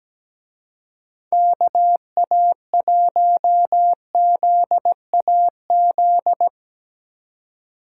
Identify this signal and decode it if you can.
Morse code, heard as a keyed tone, about 19 wpm, 700 Hz. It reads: KA1ZAZ
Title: Speed Racing - USA Call Signs 17wpm